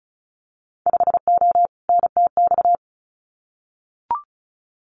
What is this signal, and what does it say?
Morse code, heard as a keyed tone, about 36 wpm, 700 Hz. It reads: 5O NTX
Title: Winter Field Day Exchanges Encore 35wpm